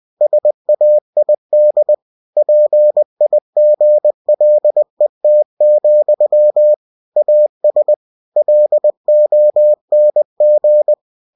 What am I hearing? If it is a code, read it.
Morse code, heard as a keyed tone, about 20 wpm, 600 Hz. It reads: SAID PIGLET, AS LONG
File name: cw-books / winnie-the-pooh / winnie-20wpm-page-0065